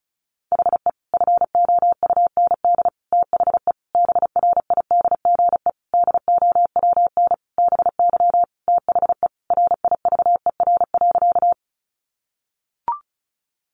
Morse code, heard as a keyed tone, about 34 wpm, 700 Hz. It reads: HE FOUND THE BRIDGE DOWN BY THE RIVER.
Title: Sentences from Top 1200 Words 35wpm